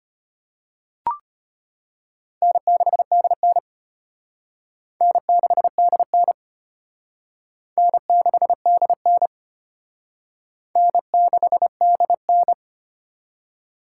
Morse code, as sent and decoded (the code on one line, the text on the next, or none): -. -.... -.. -. / -. -.... -.. -. / -. -.... -.. -. / -. -.... -.. -.
N6DN N6DN N6DN N6DN